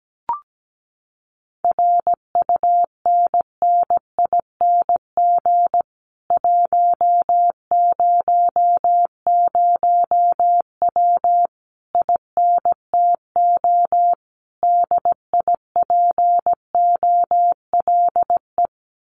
Morse code, heard as a keyed tone, about 17 wpm, 700 Hz. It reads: RUNNING 100W INTO DIPOLE
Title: Common QSO Phrases Encore 17wpm